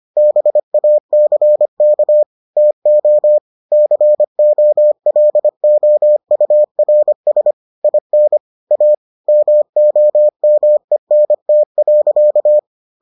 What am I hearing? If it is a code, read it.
Morse code, heard as a keyed tone, about 25 wpm, 600 Hz. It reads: BACK TO COLOURS IN A MOMENT.